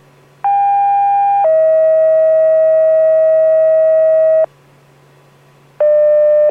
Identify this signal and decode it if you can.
Two-tone page, heard as a one-second tone, then a three-second tone, about 800 then 640 Hz. It is heard from 0.4 to 4.5 s.